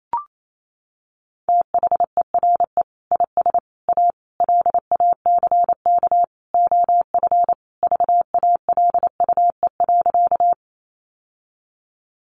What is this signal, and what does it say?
Morse code, heard as a keyed tone, about 28 wpm, 700 Hz. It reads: THERE IS A LACK OF VALUE.
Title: Sentences from Top 1100 Words 28wpm